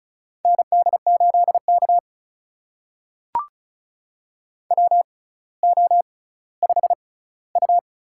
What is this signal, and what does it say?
Morse code, heard as a keyed tone, about 36 wpm, spaced as for 18 wpm, 700 Hz. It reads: ND8K WO5U
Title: USA Call Signs - 4 Characters - 6x Character Spacing - 35wpm